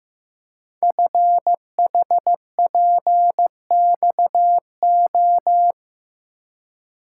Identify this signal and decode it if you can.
Morse code, heard as a keyed tone, about 15 wpm, 700 Hz. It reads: FHPXO
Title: Code Groups - 5 Letters - 15wpm